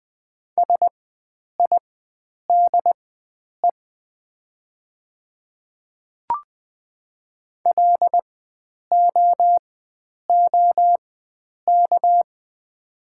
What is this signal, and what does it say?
Morse code, heard as a keyed tone, about 20 wpm, spaced as for 9 wpm, 700 Hz. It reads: SIDE LOOK
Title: 4x Character Spacing - Top 100 Words - 20wpm